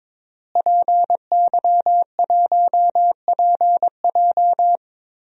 Morse code, heard as a keyed tone, about 22 wpm, 700 Hz. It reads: PY1PJ
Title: DXCC-100 Call Signs - 5 Characters - Repeated 3x - 22wpm